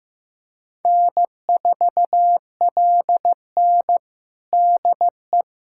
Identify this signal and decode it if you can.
Morse code, heard as a keyed tone, about 15 wpm, 700 Hz. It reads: N4LN DE
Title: USA Callsign DE Callsign - 4 Characters - 15wpm